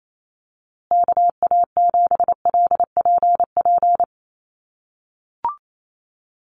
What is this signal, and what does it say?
Morse code, heard as a keyed tone, about 28 wpm, 700 Hz. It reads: KA7LPP
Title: USA Call Signs Repeated 2x 28wpm